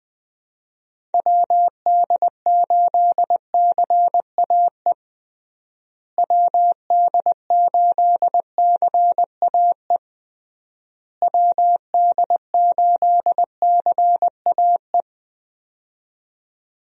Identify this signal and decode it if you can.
Morse code, heard as a keyed tone, about 20 wpm, 700 Hz. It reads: WD8CAE WD8CAE WD8CAE
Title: USA Call Signs - 6 Characters - Repeated 3x - 20wpm